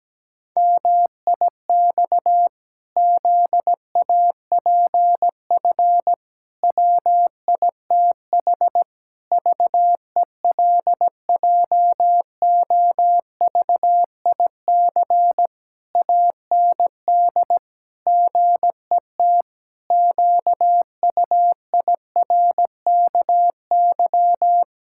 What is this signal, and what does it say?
Morse code, heard as a keyed tone, about 17 wpm, 700 Hz. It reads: MIX ZAPF WITH VELJOVIC AND GET QUIRKY
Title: Pangram Challenge v2 - 17wpm